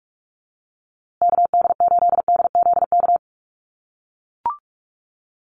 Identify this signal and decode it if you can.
Morse code, heard as a keyed tone, about 45 wpm, 700 Hz. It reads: KD8DZK
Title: USA Call Signs Repeated 2x 45wpm